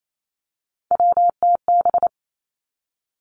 Morse code, heard as a keyed tone, about 28 wpm, 700 Hz. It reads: WTB
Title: Three Challenging Letter-Number Combinations 28wpm